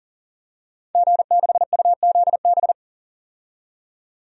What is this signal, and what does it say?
Morse code, heard as a keyed tone, about 40 wpm, 700 Hz. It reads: G6UZB
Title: DXCC-100 Call Signs - 5 Characters - 40wpm